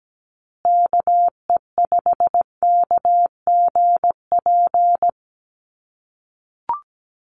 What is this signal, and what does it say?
Morse code, heard as a keyed tone, about 17 wpm, 700 Hz. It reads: KE5KGP
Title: USA Call Signs Repeated 3x 17wpm